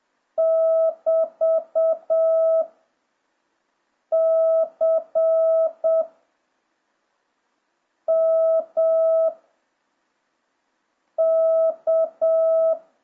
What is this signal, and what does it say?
Morse code, heard as a keyed tone, about 7 wpm, 650 Hz. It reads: <BT> C M K